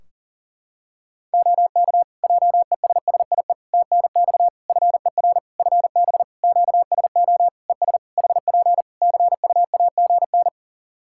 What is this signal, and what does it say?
Morse code, heard as a keyed tone, about 40 wpm, 700 Hz. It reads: OK JESSIE TNX FER FB QSO ES HP CUAGN
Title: Practice QSO - Exchange 4 - Ending - 40wpm